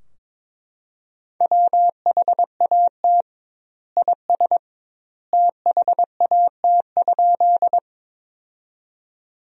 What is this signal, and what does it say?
Morse code, heard as a keyed tone, about 22 wpm, 700 Hz. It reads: WHAT IS THAT?
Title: 2x Word-Spacing - Sentences from Top 100 Words - 22wpm